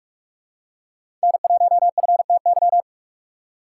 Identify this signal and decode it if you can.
Morse code, heard as a keyed tone, about 46 wpm, 700 Hz. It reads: N1FTY